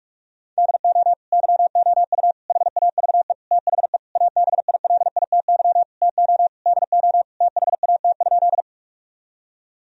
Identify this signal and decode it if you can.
Morse code, heard as a keyed tone, about 46 wpm, 700 Hz. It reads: DO YOU HAVE THE ABILITY TO DO THAT?
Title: Sentences from Top 1000 Words - Review Part 3 - 45wpm